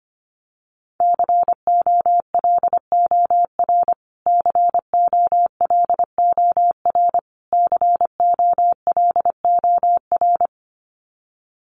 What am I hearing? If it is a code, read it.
Morse code, heard as a keyed tone, about 25 wpm, 700 Hz. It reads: COLOR COLOR COLOR
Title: Top 300 Words Repeated 25wpm